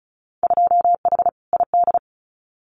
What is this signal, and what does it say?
Morse code, heard as a keyed tone, about 34 wpm, 700 Hz. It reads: 2H ID
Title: Winter Field Day Exchanges 35wpm